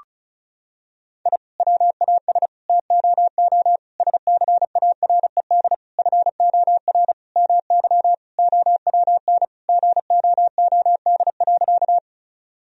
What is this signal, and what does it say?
Morse code, heard as a keyed tone, about 36 wpm, 700 Hz. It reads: I WAS TOO SCARED FOR MY OWN GOOD.
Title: Sentences from Top 1000 Words - Review Part 4 - 35wpm